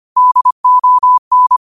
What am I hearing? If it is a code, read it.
Morse code, heard as a keyed tone, about 25 wpm, 1000 Hz. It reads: DON